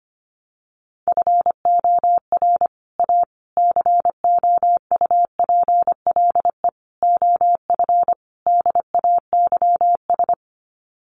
Morse code, heard as a keyed tone, about 25 wpm, 700 Hz. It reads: FOR A COUPLE OF DAYS
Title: Sets of 5 Words Encore2 25wpm